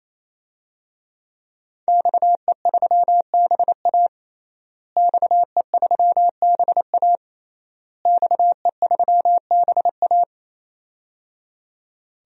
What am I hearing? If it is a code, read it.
Morse code, heard as a keyed tone, about 28 wpm, 700 Hz. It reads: XE3BA XE3BA XE3BA
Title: DXCC-100 Call Signs - 5 Characters - Repeated 3x - 28wpm